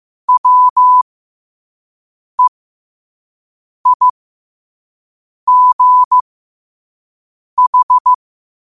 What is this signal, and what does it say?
Morse code, heard as a keyed tone, about 15 wpm, spaced as for 5 wpm, 1000 Hz. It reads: WEIGH
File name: qotdfiles / 5wpm